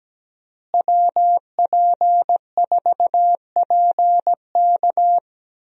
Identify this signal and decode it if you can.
Morse code, heard as a keyed tone, about 17 wpm, 700 Hz. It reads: WP4PK
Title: USA Call Signs - 5 Characters - Repeated 2x - 17wpm